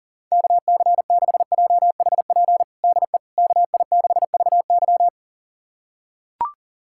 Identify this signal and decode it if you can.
Morse code, heard as a keyed tone, about 37 wpm, 700 Hz. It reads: KC6JHP DE KI6VY
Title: USA Callsign DE Callsign 40wpm